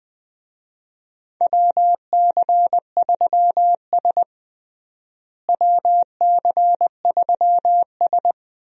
Morse code, heard as a keyed tone, about 21 wpm, 700 Hz. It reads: WC3S WC3S
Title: USA Call Signs - 4 Characters - Repeated 3x - 20wpm